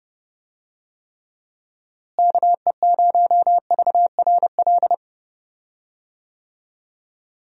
Morse code, heard as a keyed tone, about 30 wpm, 700 Hz. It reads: KE0VRL
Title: USA Call Signs - 6 Characters - Repeated 3x - 30wpm